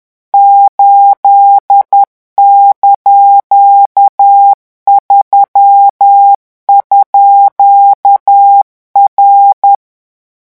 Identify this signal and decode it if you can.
Morse code, heard as a keyed tone, about 10 wpm, 800 Hz. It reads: 8)3_R
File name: morse mp3 / revision19-vitesse 10 mots